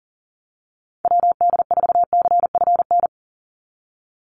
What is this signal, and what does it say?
Morse code, heard as a keyed tone, about 40 wpm, 700 Hz. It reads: WD4CFN
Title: NA SOTA Chaser Callsigns 40wpm